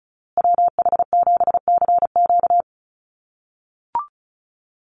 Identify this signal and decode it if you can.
Morse code, heard as a keyed tone, about 34 wpm, 700 Hz. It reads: WH7CQ
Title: USA Call Signs Repeated 2x 35wpm